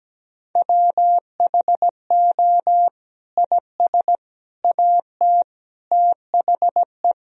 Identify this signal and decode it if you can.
Morse code, heard as a keyed tone, about 17 wpm, 700 Hz. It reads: WHO IS AT THE
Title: Sentences from Top 1000 Words - Review Part 4 - 17wpm